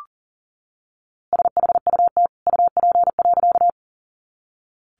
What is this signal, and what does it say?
Morse code, heard as a keyed tone, about 40 wpm, 700 Hz. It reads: SHUT UP.
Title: A Sound Of Thunder 40wpm